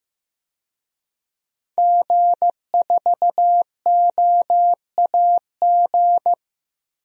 Morse code, heard as a keyed tone, about 15 wpm, 700 Hz. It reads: G4OAG
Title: DXCC-100 Call Signs - 5 Characters - 15wpm